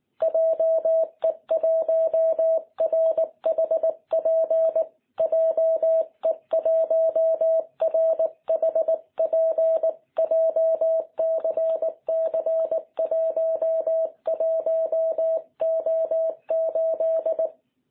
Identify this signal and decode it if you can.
Morse code, heard as a keyed tone, about 19 wpm, 650 Hz. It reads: JE1RHP JE1RHP JCC11O8
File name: jh8cbh / 2021cw